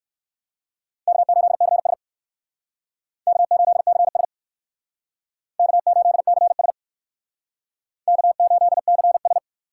Morse code, heard as a keyed tone, about 56 wpm, 700 Hz. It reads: K8CS K8CS K8CS K8CS
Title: Speed Racing - USA 1x2 Call Signs 45wpm